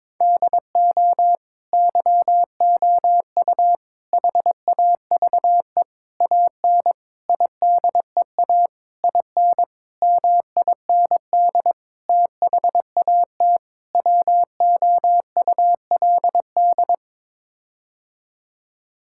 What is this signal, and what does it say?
Morse code, heard as a keyed tone, about 22 wpm, 700 Hz. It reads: DO YOU HAVE AN IDEA IN MIND THAT WOULD
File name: Sets of 9 Words - 22wpm